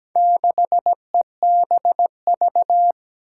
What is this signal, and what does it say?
Morse code, heard as a keyed tone, about 17 wpm, 700 Hz. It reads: 6EBV
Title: Four Challenging Letter-Number Combinations 17wpm